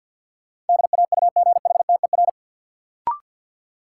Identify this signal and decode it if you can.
Morse code, heard as a keyed tone, about 53 wpm, 700 Hz. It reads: DAUGHTER
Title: Sentences from Top 1500 Words - New Word Drill - 50wpm